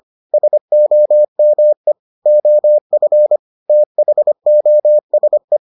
Morse code, heard as a keyed tone, about 25 wpm, 600 Hz. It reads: SOME OF THOSE